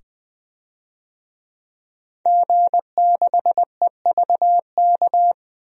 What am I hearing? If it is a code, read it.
Morse code, heard as a keyed tone, about 20 wpm, 700 Hz. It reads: G6EVK
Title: DXCC-100 Call Signs - 5 Characters - 20wpm